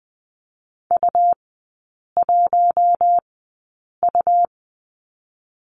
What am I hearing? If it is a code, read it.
Morse code, heard as a keyed tone, about 20 wpm, spaced as for 8 wpm, 700 Hz. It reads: U1U
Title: 107 Lesson 18e - Letter m - Three Characters 20wpm